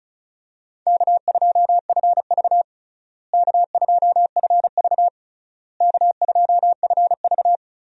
Morse code, heard as a keyed tone, about 34 wpm, 700 Hz. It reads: K2FV K2FV K2FV
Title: USA Call Signs - 4 Characters - Repeated 3x - 35wpm